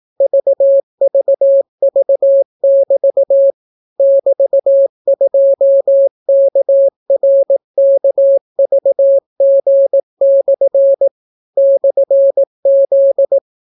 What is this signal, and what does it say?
Morse code, heard as a keyed tone, about 18 wpm, 550 Hz. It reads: VVV<BT> <BT>2KRKVG/ /Z